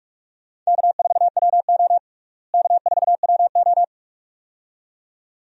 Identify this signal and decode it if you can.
Morse code, heard as a keyed tone, about 42 wpm, 700 Hz. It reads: K4WO K4WO
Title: USA Call Signs - 4 Characters - Repeated 2x - 45wpm